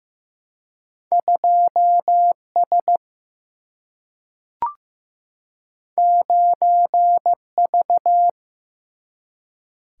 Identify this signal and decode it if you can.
Morse code, heard as a keyed tone, about 15 wpm, 700 Hz. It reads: 2S 9V
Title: Single-Letter Single-Number 15wpm